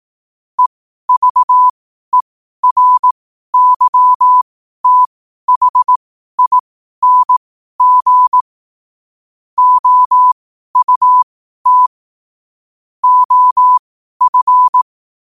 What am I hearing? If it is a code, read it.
Morse code, heard as a keyed tone, about 18 wpm, spaced as for 12 wpm, 1000 Hz. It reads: EVERYTHING OUT OF
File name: qotdfiles / 12wpm